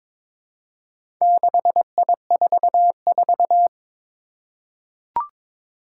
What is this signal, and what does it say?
Morse code, heard as a keyed tone, about 22 wpm, 700 Hz. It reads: 6I44
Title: Four Challenging Letter-Number Combinations 22wpm